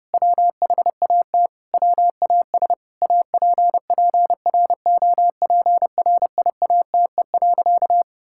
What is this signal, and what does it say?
Morse code, heard as a keyed tone, about 32 wpm, 700 Hz. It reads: WHAT WAS APPROPRIATE.
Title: Sentences from Top 1000 Words - Review Part 5 - 30wpm